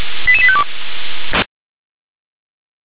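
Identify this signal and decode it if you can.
Five-tone selective call (ZVEI), heard as quick runs of five tones, tones of about 70 ms. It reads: 88852
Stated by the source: ZVEI five-tone call received on SDR.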